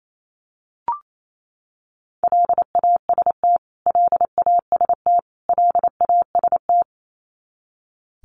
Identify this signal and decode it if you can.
Morse code, heard as a keyed tone, about 28 wpm, 700 Hz. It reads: LAST LAST LAST
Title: Top 300 Words Repeated 28wpm